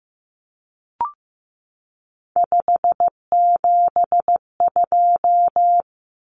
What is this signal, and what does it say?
Morse code, heard as a keyed tone, about 15 wpm, 700 Hz. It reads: 572
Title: Common Cut Numbers - 3-Digit 15wpm